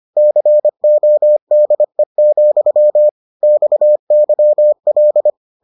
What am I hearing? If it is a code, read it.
Morse code, heard as a keyed tone, about 25 wpm, 600 Hz. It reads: CODE, XYL